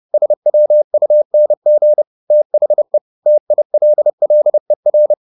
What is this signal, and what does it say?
Morse code, heard as a keyed tone, about 30 wpm, 600 Hz. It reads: SWUNG THE TILLER